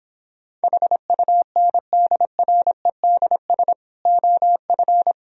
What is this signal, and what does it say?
Morse code, heard as a keyed tone, about 26 wpm, 700 Hz. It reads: HUNDREDS OF